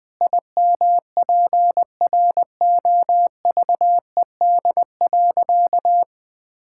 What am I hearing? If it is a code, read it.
Morse code, heard as a keyed tone, about 20 wpm, 700 Hz. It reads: IMPROVED.
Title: Sentences with 4-Word Phrases - 20wpm